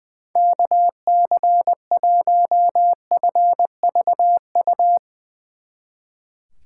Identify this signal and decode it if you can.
Morse code, heard as a keyed tone, about 21 wpm, 700 Hz. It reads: KC1FVU
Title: Speed Racing - USA 2x3 Call Signs 20wpm